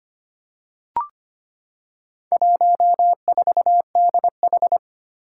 Morse code, heard as a keyed tone, about 25 wpm, 700 Hz. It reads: 14DH
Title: Four Challenging Letter-Number Combinations 25wpm